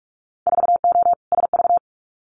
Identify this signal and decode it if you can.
Morse code, heard as a keyed tone, about 44 wpm, 700 Hz. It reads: 4O SV
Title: Winter Field Day Exchanges Encore 45wpm